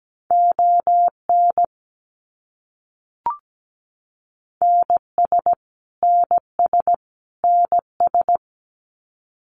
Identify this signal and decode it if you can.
Morse code, heard as a keyed tone, about 17 wpm, 700 Hz. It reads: ON NS NS NS
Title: Canadian Provinces and Territories - Repeated 3x - 17wpm